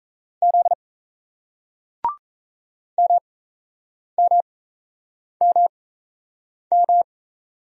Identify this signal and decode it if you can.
Morse code, heard as a keyed tone, about 38 wpm, spaced as for 8 wpm, 700 Hz. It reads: Z MMMM